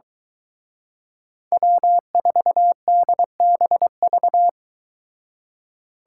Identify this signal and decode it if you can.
Morse code, heard as a keyed tone, about 22 wpm, 700 Hz. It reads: W4DBV